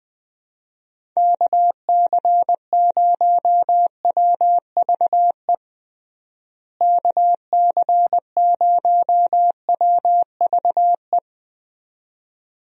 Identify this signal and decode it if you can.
Morse code, heard as a keyed tone, about 20 wpm, 700 Hz. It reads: KC0WVE KC0WVE
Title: USA Call Signs - 6 Characters - Repeated 2x - 20wpm